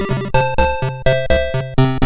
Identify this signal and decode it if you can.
There is no code.